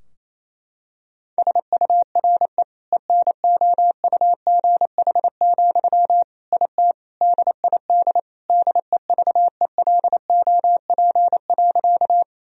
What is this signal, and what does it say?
Morse code, heard as a keyed tone, about 28 wpm, 700 Hz. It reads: SURE ENOUGH, IT DID DEVELOP.
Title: Sentences from Top 1000 Words - Review Part 2 - 28wpm